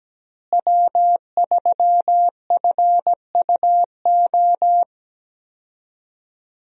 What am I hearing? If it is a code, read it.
Morse code, heard as a keyed tone, about 17 wpm, 700 Hz. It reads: W3FUO
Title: USA Call Signs - 5 Characters - Repeated 3x - 17wpm